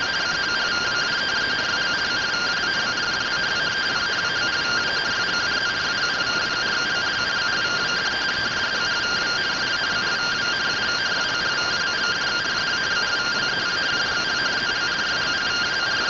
Two-tone teleprinter signal, about 170 Hz shift.
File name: findings / MR AXIAL fr1 120.gen-44 rtty rx